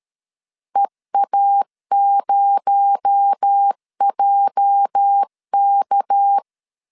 Morse code, heard as a keyed tone, about 13 wpm, 800 Hz. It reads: EA0JK
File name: CW / DET / 1 9 1500C S A 12wpm 3800